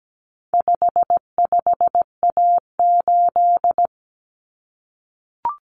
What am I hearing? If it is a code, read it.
Morse code, heard as a keyed tone, about 17 wpm, 700 Hz. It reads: 55A8
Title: Common Cut Numbers - 4-Digit 17wpm